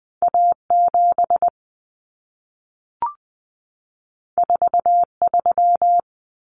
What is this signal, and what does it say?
Morse code, heard as a keyed tone, about 20 wpm, 700 Hz. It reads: A7 43
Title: Common Cut Numbers - 2-Digit 20wpm